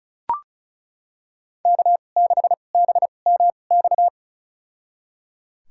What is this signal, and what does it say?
Morse code, heard as a keyed tone, about 36 wpm, spaced as for 25 wpm, 700 Hz. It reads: K6BMX
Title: USA Call Signs - 2x Character Spacing - 35wpm